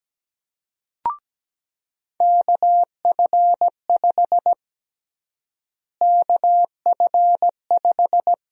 Morse code, heard as a keyed tone, about 17 wpm, 700 Hz. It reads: KF5 KF5
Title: USA Call Sign Prefixes - Repeated 2x - 17wpm